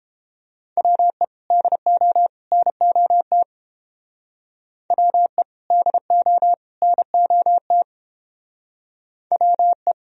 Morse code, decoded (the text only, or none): WE DO NOT WE DO NOT WE